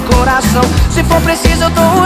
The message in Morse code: none